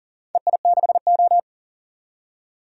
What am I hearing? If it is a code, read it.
Morse code, heard as a keyed tone, about 38 wpm, 700 Hz. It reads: EI6O
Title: DXCC-100 Call Signs - 4 Characters - 40wpm